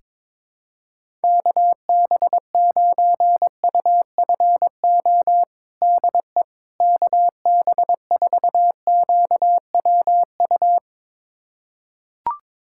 Morse code, heard as a keyed tone, about 22 wpm, 700 Hz. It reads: KB9UFO DE KB4QWU
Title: USA Callsign DE Callsign - 6 Characters - 22wpm